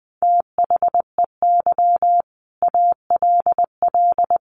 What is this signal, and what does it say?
Morse code, heard as a keyed tone, about 20 wpm, 700 Hz. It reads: THEY ALL
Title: Sentences from Top 1300 Words 20wpm